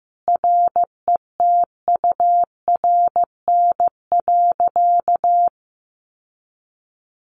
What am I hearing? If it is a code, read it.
Morse code, heard as a keyed tone, about 15 wpm, 700 Hz. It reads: RETURN.
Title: Sentences from Top 1100 Words 15wpm